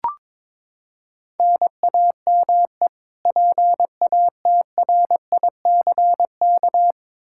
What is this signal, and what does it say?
Morse code, heard as a keyed tone, about 22 wpm, 700 Hz. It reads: NAME PATRICK
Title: Practice QSO - Exchange 1 - Beginning - 22wpm